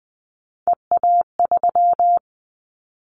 Morse code, heard as a keyed tone, about 20 wpm, 700 Hz. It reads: EA3
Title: Callsign Prefixes for Top 25 DXCC - 20wpm